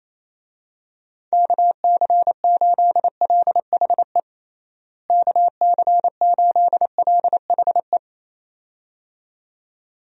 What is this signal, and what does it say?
Morse code, heard as a keyed tone, about 27 wpm, 700 Hz. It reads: KC8LHE KC8LHE
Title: USA Call Signs - 6 Characters - Repeated 2x - 28wpm